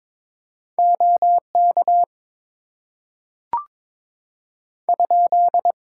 Morse code, heard as a keyed tone, about 23 wpm, 700 Hz. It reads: OK ?
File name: Top 90 QSO Elements - 8x Character Spacing - 22wpm